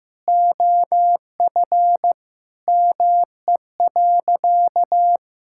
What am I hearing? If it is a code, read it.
Morse code, heard as a keyed tone, about 15 wpm, 700 Hz. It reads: OF ME.
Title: Sentences from Top 1000 Words - Review Part 6 - 15wpm